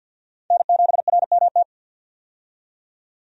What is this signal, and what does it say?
Morse code, heard as a keyed tone, about 52 wpm, 700 Hz. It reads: N7RMT
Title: USA Call Signs - 5 Characters - 50wpm